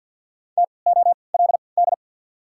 Morse code, heard as a keyed tone, about 53 wpm, spaced as for 29 wpm, 690 Hz. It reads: TOLD
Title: 3x Character Spacing - Top 500 Words - 50wpm